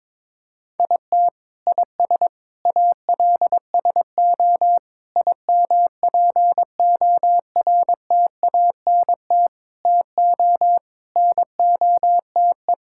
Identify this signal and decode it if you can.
Morse code, heard as a keyed tone, about 21 wpm, 700 Hz. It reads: IT IS ALSO IMPORTANT TO NOTE
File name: practice / 22wpm / Sets of 8 Words - 22wpm